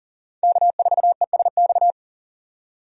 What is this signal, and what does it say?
Morse code, heard as a keyed tone, about 40 wpm, 700 Hz. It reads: K4ESX
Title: USA Call Signs - 5 Characters - 40wpm